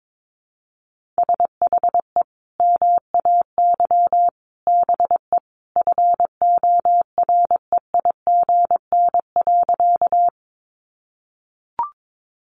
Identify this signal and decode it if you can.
Morse code, heard as a keyed tone, about 22 wpm, 700 Hz. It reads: SHE MAY BE FOREIGN.
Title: Sentences from Top 1400 Words 22wpm